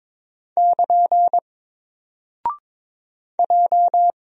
Morse code, heard as a keyed tone, about 23 wpm, 700 Hz. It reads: <KN> J
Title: Single Character (Letter:Number:Punctuation:Prosign) 22wpm